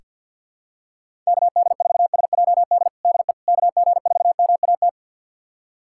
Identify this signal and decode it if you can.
Morse code, heard as a keyed tone, about 46 wpm, 700 Hz. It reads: KD4SJD DE KG4MAT
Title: USA Callsign DE Callsign 50wpm